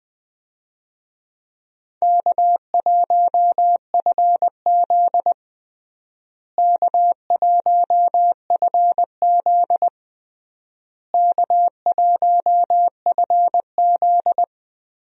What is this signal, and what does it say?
Morse code, heard as a keyed tone, about 20 wpm, 700 Hz. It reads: K1FZ K1FZ K1FZ
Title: USA Call Signs - 4 Characters - Repeated 3x - 20wpm